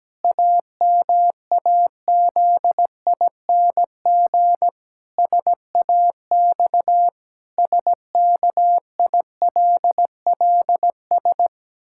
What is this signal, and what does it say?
Morse code, heard as a keyed tone, about 17 wpm, 700 Hz. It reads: AMAZING SAX SKILLS
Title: Pangram Challenge v2 - 17wpm